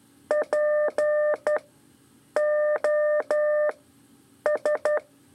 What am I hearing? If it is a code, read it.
Morse code, heard as a keyed tone, about 12 wpm, 600 Hz. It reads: P O S
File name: 3 lezione CW 2 AIR